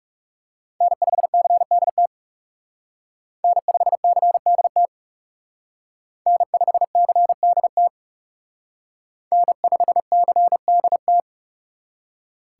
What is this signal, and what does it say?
Morse code, heard as a keyed tone, about 34 wpm, 700 Hz. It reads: N5CDT N5CDT N5CDT N5CDT
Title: Speed Racing - USA Call Signs 30wpm